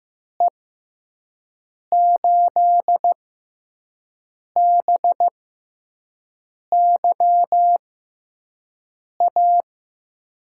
Morse code, heard as a keyed tone, about 15 wpm, spaced as for 5 wpm, 700 Hz. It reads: E8BYA
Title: USA Call Signs - 6 Characters - 6x Character Spacing - 15wpm